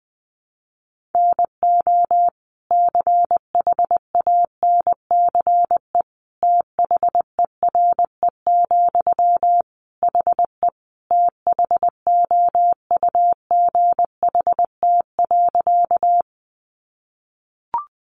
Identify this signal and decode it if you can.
Morse code, heard as a keyed tone, about 20 wpm, 700 Hz. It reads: NO CHANCE THERE, HE THOUGHT.
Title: Sentences from Top 800 Words 20wpm